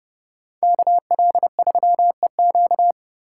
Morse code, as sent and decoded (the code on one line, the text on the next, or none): -.- .-.. ...-- . --.-
KL3EQ